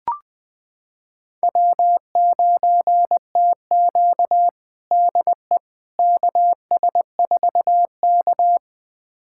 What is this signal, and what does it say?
Morse code, heard as a keyed tone, about 21 wpm, 700 Hz. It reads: W9TQ DE KS4K
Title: USA Callsign DE Callsign - 4 Characters - 20wpm